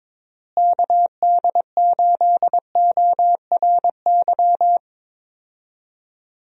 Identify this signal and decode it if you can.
Morse code, heard as a keyed tone, about 22 wpm, 700 Hz. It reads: KD8ORY
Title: USA Call Signs - 6 Characters - Repeated 2x - 22wpm